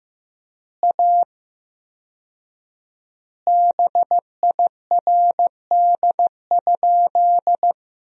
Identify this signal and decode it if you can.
Morse code, heard as a keyed tone, about 15 wpm, 700 Hz. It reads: A BIRD?
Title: 4x Word-Spacing - Sentences from Top 400 Words - 15wpm